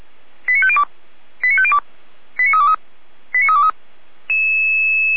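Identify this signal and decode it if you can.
Five-tone selective call (ZVEI), heard as quick runs of five tones, tones of about 70 ms. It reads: 89502 89502 89323 89323
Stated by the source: ZVEI five-tone selective call 89099 received via SDR, from the server of Feuerwehr Orlamünde.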